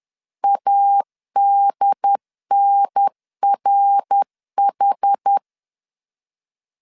Morse code, heard as a keyed tone, about 10 wpm, 800 Hz. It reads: ADNRH
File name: CW / DET / 1 3 1500C S C 10wpm 3335